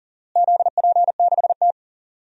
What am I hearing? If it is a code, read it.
Morse code, heard as a keyed tone, about 40 wpm, 700 Hz. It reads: ZP6T
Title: DXCC-100 Call Signs - 4 Characters - Repeated 2x - 40wpm